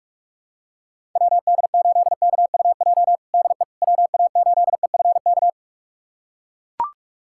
Morse code, heard as a keyed tone, about 43 wpm, 700 Hz. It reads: WD8KUJ DE WA8EFK
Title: USA Callsign DE Callsign 45wpm